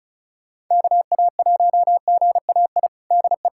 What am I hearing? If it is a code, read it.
Morse code, heard as a keyed tone, about 36 wpm, 700 Hz. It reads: KA1GAI DE
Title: USA Callsign DE Callsign - 6 Characters - 35wpm